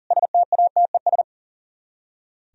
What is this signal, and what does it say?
Morse code, heard as a keyed tone, about 50 wpm, 700 Hz. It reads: STATES
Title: Speed-Racing - Sets of 5 Words 40wpm